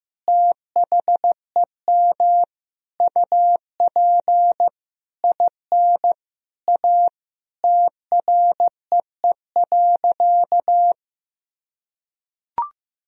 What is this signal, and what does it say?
Morse code, heard as a keyed tone, about 15 wpm, 700 Hz. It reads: THEM UP IN A TREE.
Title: Sentences from Top 1000 Words - Review Part 1 - 15wpm